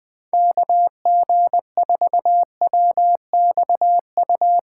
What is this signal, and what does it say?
Morse code, heard as a keyed tone, about 20 wpm, 700 Hz. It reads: KG4WXU
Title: POTA QSO Practice - 20wpm